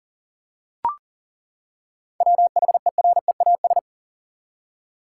Morse code, heard as a keyed tone, about 40 wpm, 700 Hz. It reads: WHEREAS
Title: Seven Letter Words - 40wpm